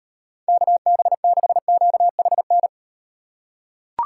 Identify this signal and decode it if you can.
Morse code, heard as a keyed tone, about 38 wpm, 700 Hz. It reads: KB6QHN